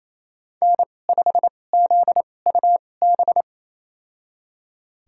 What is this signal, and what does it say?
Morse code, heard as a keyed tone, about 27 wpm, spaced as for 20 wpm, 700 Hz. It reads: N5ZUB
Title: USA Call Signs - 2x Character Spacing - 28wpm